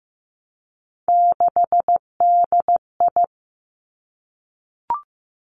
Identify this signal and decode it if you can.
Morse code, heard as a keyed tone, about 15 wpm, 700 Hz. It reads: 6DI
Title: Three Challenging Letter-Number Combinations 15wpm